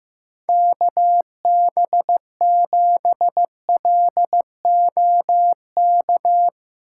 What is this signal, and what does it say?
Morse code, heard as a keyed tone, about 15 wpm, 700 Hz. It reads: KB7LOK
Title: USA Call Signs - 6 Characters - 4x Character Spacing - 15wpm